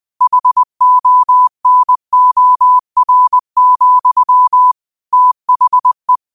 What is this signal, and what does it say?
Morse code, heard as a keyed tone, about 20 wpm, 1000 Hz. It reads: HONOR, THE
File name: qotdfiles / 20wpm